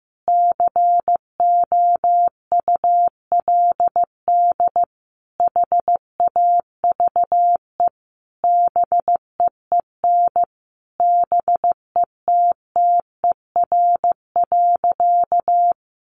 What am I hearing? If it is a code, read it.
Morse code, heard as a keyed tone, about 15 wpm, 700 Hz. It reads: COULD HAVE BEEN BETTER.
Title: Sentences from Top 500 Words 15wpm